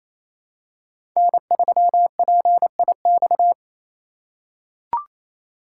Morse code, heard as a keyed tone, about 28 wpm, 700 Hz. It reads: N3PIX
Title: USA Call Signs - 5 Characters - 4x Character Spacing - 28wpm